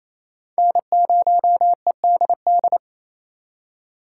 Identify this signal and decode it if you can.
Morse code, heard as a keyed tone, about 36 wpm, 700 Hz. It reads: N0EDD